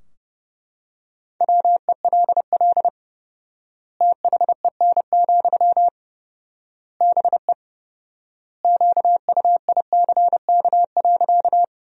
Morse code, heard as a keyed tone, about 30 wpm, 700 Hz. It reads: WELL THEN, BE QUICK.
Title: 4x Word-Spacing - Sentences from Top 500 Words - 30wpm